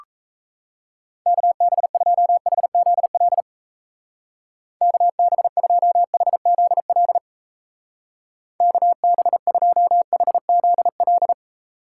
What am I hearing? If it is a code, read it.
Morse code, heard as a keyed tone, about 33 wpm, 700 Hz. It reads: KB2HZL KB2HZL KB2HZL
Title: Speed Racing - USA Call Signs 28wpm